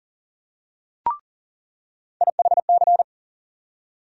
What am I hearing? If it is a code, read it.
Morse code, heard as a keyed tone, about 40 wpm, 700 Hz. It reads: IHC
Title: USA Call Sign Postfixes - 3 Letters 40wpm